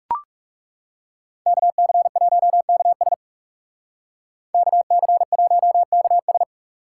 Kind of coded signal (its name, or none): Morse code